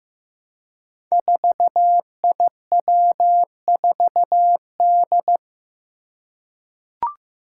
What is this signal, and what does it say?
Morse code, heard as a keyed tone, about 15 wpm, 700 Hz. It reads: 4IW4D
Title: Five Challenging Letter-Number Combinations 15wpm